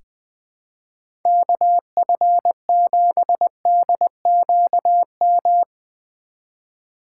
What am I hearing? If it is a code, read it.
Morse code, heard as a keyed tone, about 20 wpm, 700 Hz. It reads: KF7DQM
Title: USA Call Signs - 6 Characters - 2x Character Spacing - 20wpm